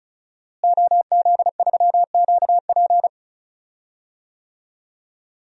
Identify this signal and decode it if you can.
Morse code, heard as a keyed tone, about 36 wpm, 700 Hz. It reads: OZ3QP
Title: DXCC-100 Call Signs - 5 Characters - Repeated 3x - 35wpm